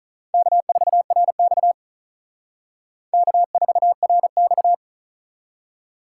Morse code, heard as a keyed tone, about 44 wpm, 700 Hz. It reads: K4RX K4RX